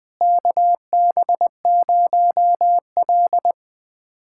Morse code, heard as a keyed tone, about 21 wpm, 700 Hz. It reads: KB0L
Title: USA Callsign DE Callsign - 4 Characters - 20wpm